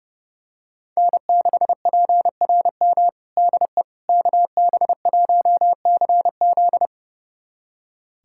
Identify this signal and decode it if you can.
Morse code, heard as a keyed tone, about 30 wpm, 700 Hz. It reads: N6PRM DE KB1CZ
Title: USA Callsign DE Callsign - 5 Characters - 30wpm